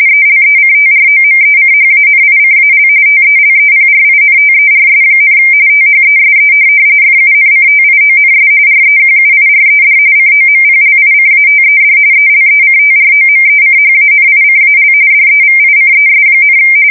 RTTY, about 45.45 baud. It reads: ELECTROMECHANICAL TELEPRINTERS, THESE FUNCTIONS REQUIRED COMPLICATED ELECTROMECHANICAL DEVICES,
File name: fsk170 / teletype-RTTY